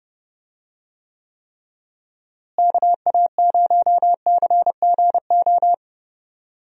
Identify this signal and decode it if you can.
Morse code, heard as a keyed tone, about 30 wpm, 700 Hz. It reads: KA0CGO
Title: USA Call Signs - 6 Characters - Repeated 3x - 30wpm